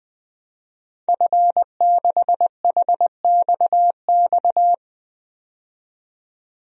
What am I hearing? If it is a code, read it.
Morse code, heard as a keyed tone, about 20 wpm, 700 Hz. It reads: F6HXX
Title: DXCC-100 Call Signs - 5 Characters - 20wpm